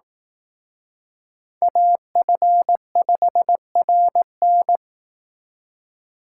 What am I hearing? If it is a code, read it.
Morse code, heard as a keyed tone, about 18 wpm, 700 Hz. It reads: AF5RN
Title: Speed Racing - USA 2x2 Call Signs 15wpm